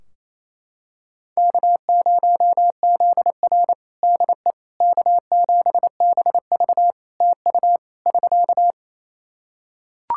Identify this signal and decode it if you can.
Morse code, heard as a keyed tone, about 28 wpm, 700 Hz. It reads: K0ZR DE K7BV TU <SK>
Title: Practice QSO - Exchange 4 - Ending - 28wpm